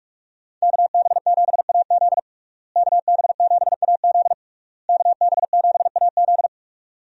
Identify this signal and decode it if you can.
Morse code, heard as a keyed tone, about 44 wpm, 700 Hz. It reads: KB7AZ KB7AZ KB7AZ
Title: USA Call Signs - 5 Characters - Repeated 3x - 45wpm